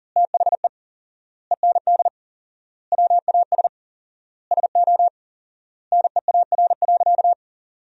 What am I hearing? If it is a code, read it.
Morse code, heard as a keyed tone, about 40 wpm, 700 Hz. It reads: THE END WAS SO NEAR.
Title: 4x Word-Spacing - Sentences from Top 300 Words - 40wpm